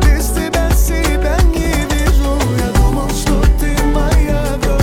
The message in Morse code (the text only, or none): none